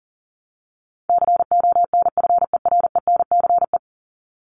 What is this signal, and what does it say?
Morse code, heard as a keyed tone, about 37 wpm, 700 Hz. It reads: CONFERENCE
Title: Top 1751-2000 Words 40wpm